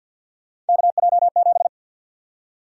Morse code, heard as a keyed tone, about 52 wpm, 700 Hz. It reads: KJ7
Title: USA Call Sign Prefixes - Repeated 2x - 50wpm